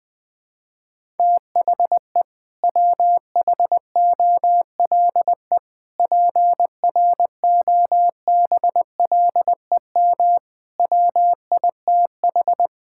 Morse code, heard as a keyed tone, about 20 wpm, 700 Hz. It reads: THE WHOLE PROBLEM WITH
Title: Sentences from Top 1000 Words - Review Part 6 - 20wpm